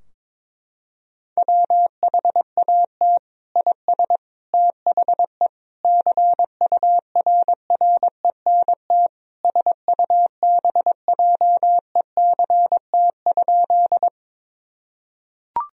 Morse code, heard as a keyed tone, about 22 wpm, 700 Hz. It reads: WHAT IS THE CURRENT SUBJECT?
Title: Sentences from Top 1000 Words - Review Part 4 - 22wpm